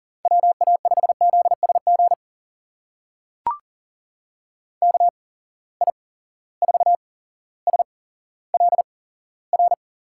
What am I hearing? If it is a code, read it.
Morse code, heard as a keyed tone, about 38 wpm, spaced as for 17 wpm, 700 Hz. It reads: WA5ZSG KI4SLR
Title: USA Call Signs - 6 Characters - 8x Character Spacing - 40wpm